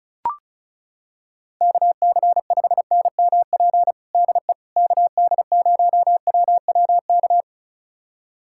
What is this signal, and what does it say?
Morse code, heard as a keyed tone, about 36 wpm, 700 Hz. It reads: KC5NMP DE KD0WWK
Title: USA Callsign DE Callsign 35wpm